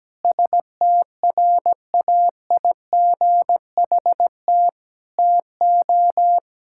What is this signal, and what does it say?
Morse code, heard as a keyed tone, about 17 wpm, 700 Hz. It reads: STRAIGHT TO
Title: Sentences from Top 1000 Words - Review Part 6 - 17wpm